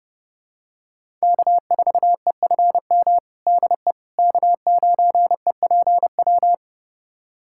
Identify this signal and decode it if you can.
Morse code, heard as a keyed tone, about 30 wpm, 700 Hz. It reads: K4EFM DE K9EPW
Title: USA Callsign DE Callsign - 5 Characters - 30wpm